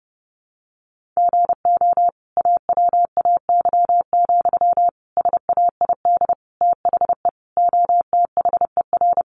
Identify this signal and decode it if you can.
Morse code, heard as a keyed tone, about 30 wpm, 700 Hz. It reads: GO AWAY, SAID THE OTHER
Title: Sentences from Top 1500 Words 30wpm